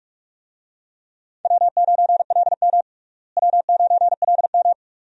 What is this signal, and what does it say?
Morse code, heard as a keyed tone, about 42 wpm, 700 Hz. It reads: W9LM W9LM
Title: USA Call Signs - 4 Characters - Repeated 2x - 45wpm